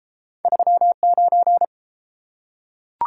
Morse code, heard as a keyed tone, about 30 wpm, 700 Hz. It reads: 39